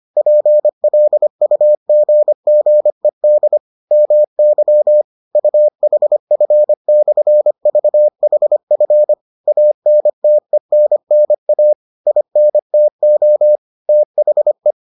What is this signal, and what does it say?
Morse code, heard as a keyed tone, about 25 wpm, 600 Hz. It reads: PLUGGED MY UHF/VHF ANTENNA INTO THE